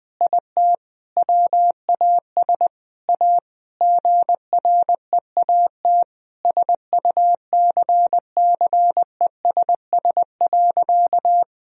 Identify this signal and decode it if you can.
Morse code, heard as a keyed tone, about 21 wpm, 700 Hz. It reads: IT WAS A GREAT SUCCESS.
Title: Sentences from Top 1000 Words - Review Part 4 - 20wpm